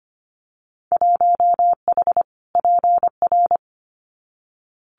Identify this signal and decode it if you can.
Morse code, heard as a keyed tone, about 25 wpm, 700 Hz. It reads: 1H PR
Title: Winter Field Day Exchanges Encore 25wpm